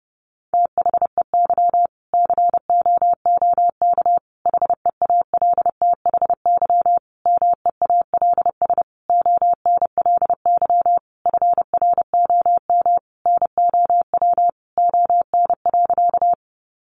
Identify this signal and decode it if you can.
Morse code, heard as a keyed tone, about 30 wpm, 700 Hz. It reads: THEY COOK HEALTHY MEALS ONLY FROM NOW ON.
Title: Easy Phrases 30wpm